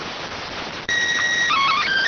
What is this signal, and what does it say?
SSTV (slow-scan television), PD 290.